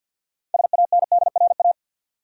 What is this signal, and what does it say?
Morse code, heard as a keyed tone, about 52 wpm, 700 Hz. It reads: SANDRA
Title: USA Common Names - 50wpm